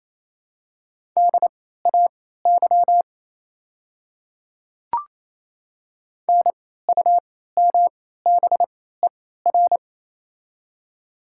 Morse code, heard as a keyed tone, about 28 wpm, spaced as for 16 wpm, 700 Hz. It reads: DAY NUMBER
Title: 3x Character Spacing - Top 100 Words - 28wpm